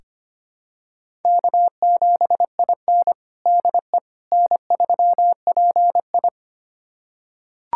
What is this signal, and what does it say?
Morse code, heard as a keyed tone, about 25 wpm, 700 Hz. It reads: K7IN DE N3PI
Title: USA Callsign DE Callsign - 4 Characters - 25wpm